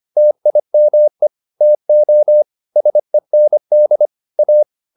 Morse code, heard as a keyed tone, about 25 wpm, 600 Hz. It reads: TIME TO SEND A